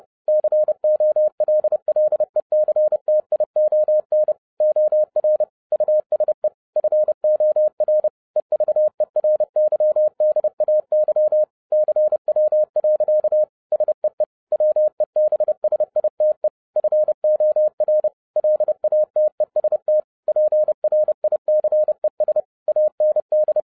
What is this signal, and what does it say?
Morse code, heard as a keyed tone, about 30 wpm, 610 Hz. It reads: COLLECTION OR USE FOR EVERYDAY CW. SEE WEBSITE FOR LATEST PRICES AND
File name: 2011-10-mp3 / n10-Czech Army Morse Keys 30wpm